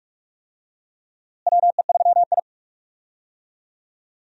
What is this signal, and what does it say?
Morse code, heard as a keyed tone, about 43 wpm, 700 Hz. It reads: WE3I
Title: USA Call Signs - 4 Characters - Repeated 3x - 45wpm